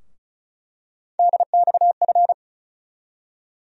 Morse code, heard as a keyed tone, about 36 wpm, 700 Hz. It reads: DXF
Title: USA Call Sign Postfixes - 3 Letters 35wpm